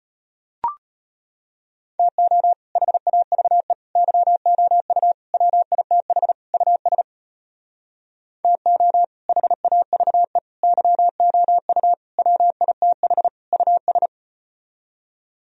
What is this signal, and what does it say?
Morse code, heard as a keyed tone, about 37 wpm, 700 Hz. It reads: TO HAVE YOU WITH US TO HAVE YOU WITH US